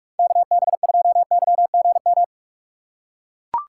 Morse code, heard as a keyed tone, about 41 wpm, 700 Hz. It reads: KB2YGM
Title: USA Call Signs - 6 Characters - 6x Character Spacing - 45wpm